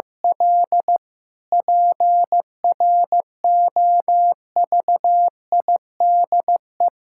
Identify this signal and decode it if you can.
Morse code, heard as a keyed tone, about 15 wpm, 700 Hz. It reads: L PROVIDE
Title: Sentences from Top 1000 Words - Review Part 6 - 15wpm